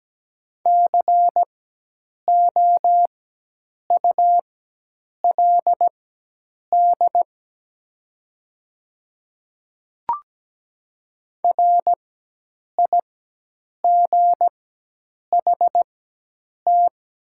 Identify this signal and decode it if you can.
Morse code, heard as a keyed tone, about 17 wpm, spaced as for 8 wpm, 700 Hz. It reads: COULD RIGHT
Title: 4x Character Spacing - Top 200 Words - 17wpm